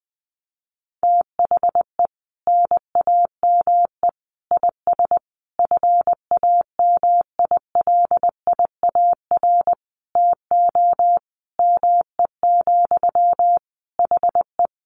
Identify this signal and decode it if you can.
Morse code, heard as a keyed tone, about 20 wpm, 700 Hz. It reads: THE NAME IS FAMILIAR TO ME, HE
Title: Sentences from Top 1000 Words 20wpm